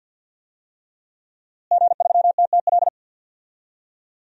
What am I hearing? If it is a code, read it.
Morse code, heard as a keyed tone, about 47 wpm, 700 Hz. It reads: G3TTL
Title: DXCC-100 Call Signs - 5 Characters - Repeated 3x - 50wpm